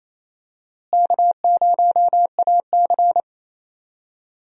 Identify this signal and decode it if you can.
Morse code, heard as a keyed tone, about 28 wpm, 700 Hz. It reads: K0AC
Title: USA Call Signs - 4 Characters - Repeated 2x - 28wpm